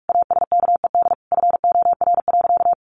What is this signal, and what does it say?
Morse code, heard as a keyed tone, about 44 wpm, 700 Hz. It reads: ASKED FOR.
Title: A Sound Of Thunder 45wpm